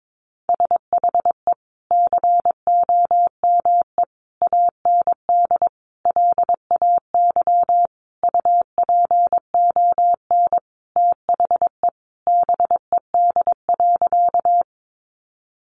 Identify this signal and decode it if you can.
Morse code, heard as a keyed tone, about 22 wpm, 700 Hz. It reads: SHE COME AND LAY UPON THE BED.
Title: Sentences from Top 1100 Words 22wpm